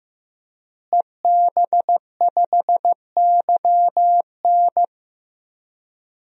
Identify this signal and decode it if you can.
Morse code, heard as a keyed tone, about 15 wpm, 700 Hz. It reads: EB5YN
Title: DXCC-100 Call Signs - 5 Characters - Repeated 2x - 15wpm